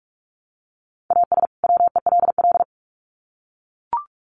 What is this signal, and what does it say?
Morse code, heard as a keyed tone, about 44 wpm, 700 Hz. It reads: AS WELL
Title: Sets of 2 Words 45wpm